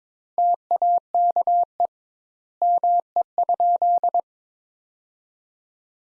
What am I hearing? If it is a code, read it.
Morse code, heard as a keyed tone, about 22 wpm, 700 Hz. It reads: TAKE ME?
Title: Sentences Top 200 Words - 22wpm